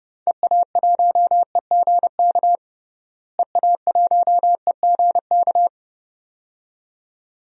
Morse code, heard as a keyed tone, about 29 wpm, 700 Hz. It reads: EA1EGK EA1EGK
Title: DXCC-100 Call Signs - 6 Characters - Repeated 2x - 30wpm